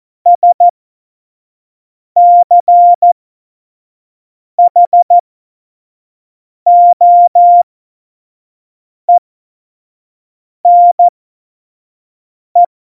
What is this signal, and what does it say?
Morse code, heard as a keyed tone, about 14 wpm, spaced as for 5 wpm, 700 Hz. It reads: SCHOENE
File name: Morsekurs / Witze / Witz 08. Mai 2018, 25 F70